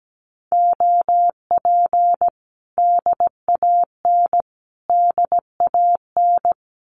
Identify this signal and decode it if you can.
Morse code, heard as a keyed tone, about 17 wpm, 700 Hz. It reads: OP DAN DAN
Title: Common QSO Phrases Encore 17wpm